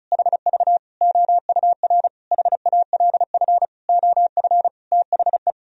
Morse code, heard as a keyed tone, about 36 wpm, 700 Hz. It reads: HV OUR HALF OF THE
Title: Sentences from Top 400 Words and Abbreviations 35wpm